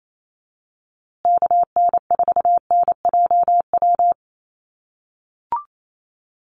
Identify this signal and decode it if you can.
Morse code, heard as a keyed tone, about 28 wpm, 700 Hz. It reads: KN4NJW
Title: USA Call Signs Repeated 2x 28wpm